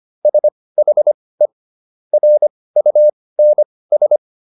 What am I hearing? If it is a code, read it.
Morse code, heard as a keyed tone, about 25 wpm, spaced as for 18 wpm, 600 Hz. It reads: SHE RUNS